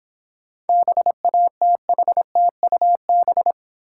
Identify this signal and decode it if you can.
Morse code, heard as a keyed tone, about 26 wpm, 700 Hz. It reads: BATHTUB